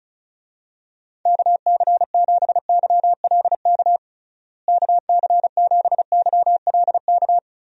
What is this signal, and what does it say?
Morse code, heard as a keyed tone, about 36 wpm, 700 Hz. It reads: KC7YLK KC7YLK
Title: USA Call Signs - 6 Characters - Repeated 2x - 35wpm